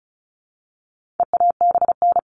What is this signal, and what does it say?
Morse code, heard as a keyed tone, about 34 wpm, 700 Hz. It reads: EABN
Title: All Cut Numbers - 4-Digit 35wpm